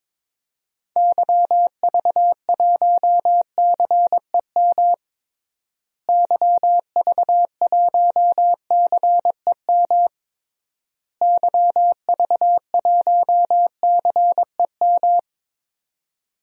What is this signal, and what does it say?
Morse code, heard as a keyed tone, about 22 wpm, 700 Hz. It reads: YV1CEM YV1CEM YV1CEM
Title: DXCC-100 Call Signs - 6 Characters - Repeated 3x - 22wpm